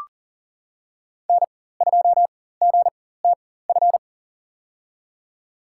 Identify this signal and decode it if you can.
Morse code, heard as a keyed tone, about 40 wpm, spaced as for 19 wpm, 700 Hz. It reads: N2GTF
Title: USA Call Signs - 5 Characters - 4x Character Spacing - 40wpm